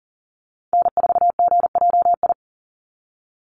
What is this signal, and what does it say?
Morse code, heard as a keyed tone, about 42 wpm, 700 Hz. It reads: N4GJI
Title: USA Call Signs 40wpm